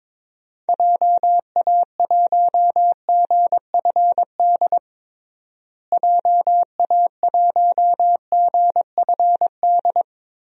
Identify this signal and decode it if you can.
Morse code, heard as a keyed tone, about 22 wpm, 700 Hz. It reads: JA1GFD JA1GFD
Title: DXCC-100 Call Signs - 6 Characters - Repeated 3x - 22wpm